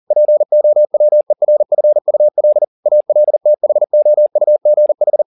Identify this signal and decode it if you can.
Morse code, heard as a keyed tone, about 40 wpm, 600 Hz. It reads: POWERFUL ALTHOUGH